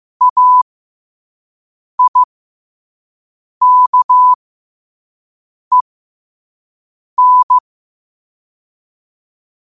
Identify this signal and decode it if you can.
Morse code, heard as a keyed tone, about 15 wpm, spaced as for 5 wpm, 1000 Hz. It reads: AIKEN
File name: qotdfiles / 5wpm